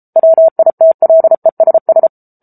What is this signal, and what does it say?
Morse code, heard as a keyed tone, about 34 wpm, 650 Hz. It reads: WITLESS
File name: Book Two / 07-Chapter 7 - 33 WPM